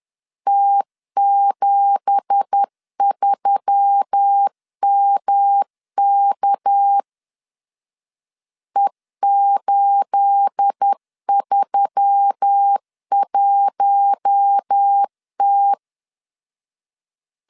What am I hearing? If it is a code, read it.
Morse code, heard as a keyed tone, about 10 wpm, 800 Hz. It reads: T73MK E831T
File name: CW / DET / 1 9 1500C S B 10wpm 4439